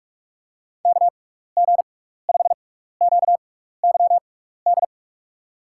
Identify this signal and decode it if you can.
Morse code, heard as a keyed tone, about 44 wpm, spaced as for 15 wpm, 700 Hz. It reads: KG5QYD
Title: USA Call Signs - 6 Characters - 6x Character Spacing - 45wpm